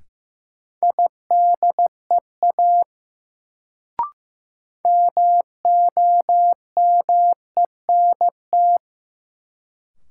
Morse code, heard as a keyed tone, about 15 wpm, 700 Hz. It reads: IDEA MOMENT
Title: Sets of 3 Words - New Word Drill - 15wpm